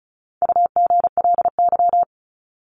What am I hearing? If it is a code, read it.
Morse code, heard as a keyed tone, about 34 wpm, 700 Hz. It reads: UGLY
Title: Top 1751-2000 Words 35wpm